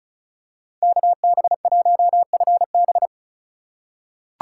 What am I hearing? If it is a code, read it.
Morse code, heard as a keyed tone, about 34 wpm, 700 Hz. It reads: KB1FB
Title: USA Call Signs - 5 Characters - 4x Character Spacing - 35wpm